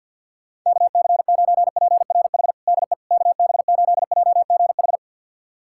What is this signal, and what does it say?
Morse code, heard as a keyed tone, about 52 wpm, 700 Hz. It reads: KC9PRH DE KB8JGH
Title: USA Callsign DE Callsign - 6 Characters - 50wpm